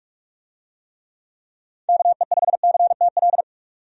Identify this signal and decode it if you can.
Morse code, heard as a keyed tone, about 46 wpm, 700 Hz. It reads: KE5CTL
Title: USA Call Signs - 6 Characters - 45wpm